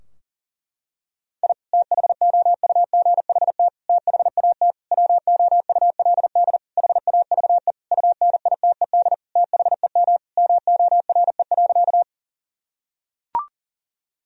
Morse code, heard as a keyed tone, about 37 wpm, 700 Hz. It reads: I THOUGHT THAT WOULD HAVE UNITED THEM MORE.
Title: Sentences from Top 1000 Words - Review Part 4 - 40wpm